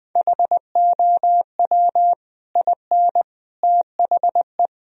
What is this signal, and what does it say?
Morse code, heard as a keyed tone, about 20 wpm, 700 Hz. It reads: HOW IN THE
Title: Sentences from Top 1000 Words - Review Part 4 - 20wpm